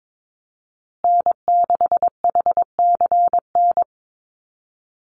Morse code, heard as a keyed tone, about 22 wpm, 700 Hz. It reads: N6HCN
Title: Call Signs Repeated 2x - 22wpm